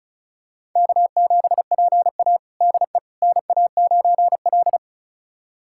Morse code, heard as a keyed tone, about 36 wpm, 700 Hz. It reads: K7PA DE NA9L
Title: USA Callsign DE Callsign - 4 Characters - 35wpm